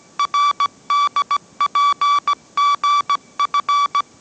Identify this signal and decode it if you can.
Morse code, heard as a keyed tone, about 17 wpm, 1150 Hz. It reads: RDPGF
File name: mrx / mrx080 18wpm